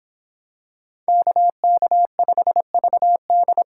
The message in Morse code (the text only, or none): KK5VD